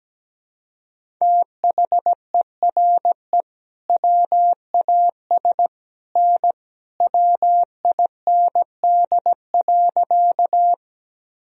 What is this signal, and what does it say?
Morse code, heard as a keyed tone, about 17 wpm, 700 Hz. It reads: THERE WAS N WIND.
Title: Sentences from Top 400 Words and Abbreviations 17wpm